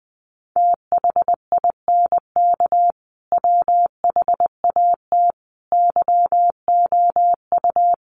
Morse code, heard as a keyed tone, about 20 wpm, 700 Hz. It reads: THINK WHAT YOU
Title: Sentences from Top 200 Words 20wpm